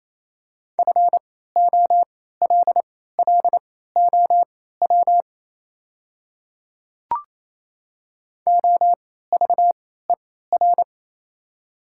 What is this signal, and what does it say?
Morse code, heard as a keyed tone, about 28 wpm, spaced as for 16 wpm, 700 Hz. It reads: FOLLOW OVER
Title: 3x Character Spacing - Top 300 Words - 28wpm